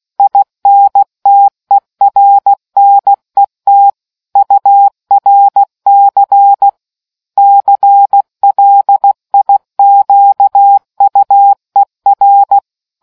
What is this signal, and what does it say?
Morse code, heard as a keyed tone, about 16 wpm, 800 Hz. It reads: INTERNET URC CLIQUER